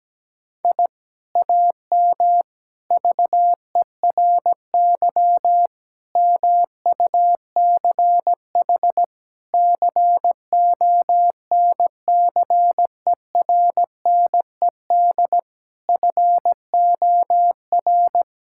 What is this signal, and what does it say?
Morse code, heard as a keyed tone, about 17 wpm, 700 Hz. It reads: I AM VERY MUCH CONCERNED FOR
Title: Sentences from Top 1000 Words - Review Part 6 - 17wpm